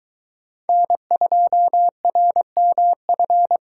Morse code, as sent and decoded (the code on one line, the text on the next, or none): -. ..--- .-. -- ..-.
N2RMF